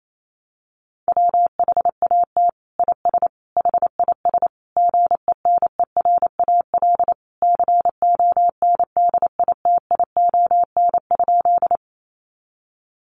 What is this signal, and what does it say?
Morse code, heard as a keyed tone, about 28 wpm, 700 Hz. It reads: WHAT IS HIS GENERAL CONDITION?
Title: Sentences from Top 1000 Words 28wpm